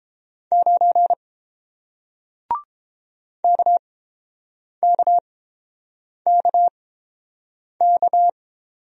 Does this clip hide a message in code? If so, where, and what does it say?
Yes: Morse code from 0.5 to 8.3 s, heard as a keyed tone, about 31 wpm, spaced as for 7 wpm, 700 Hz. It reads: 9 KKKK